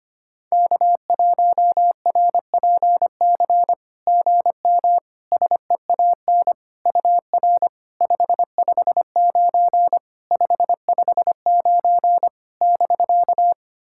Morse code, heard as a keyed tone, about 25 wpm, 690 Hz. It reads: K1RPC GM SEAN UR 559 559 <BK>
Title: SOTA QSO Practice 25wpm